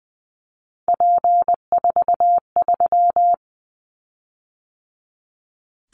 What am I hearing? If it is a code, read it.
Morse code, heard as a keyed tone, about 20 wpm, 700 Hz. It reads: P43
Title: Callsign Prefixes for Top 100 DXCC - 20wpm